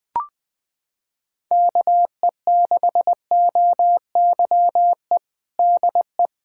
Morse code, heard as a keyed tone, about 20 wpm, 700 Hz. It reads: KE6OYE DE
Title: Practice QSO - Exchange 4 - Ending - 20wpm